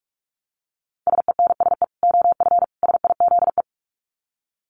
Morse code, heard as a keyed tone, about 44 wpm, 700 Hz. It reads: SENSE OF SIZE
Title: Dit Challenge 45wpm